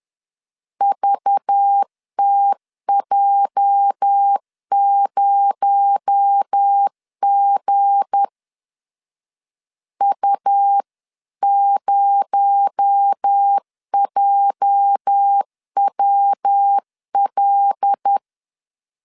Morse code, heard as a keyed tone, about 10 wpm, 800 Hz. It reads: VTJ0G U0JWL
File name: LEZIONI / 8 YJ45 / 8 YJ 10wpm 2651